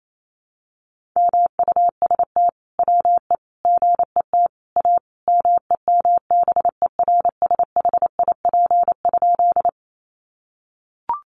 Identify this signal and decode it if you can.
Morse code, heard as a keyed tone, about 28 wpm, 700 Hz. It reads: MUST WE GET A MEMBERSHIP?
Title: Sentences from Top 1400 Words 28wpm